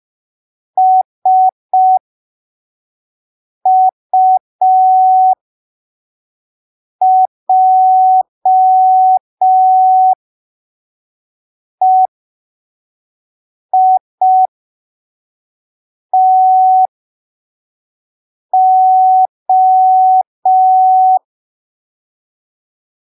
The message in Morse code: SUJEITO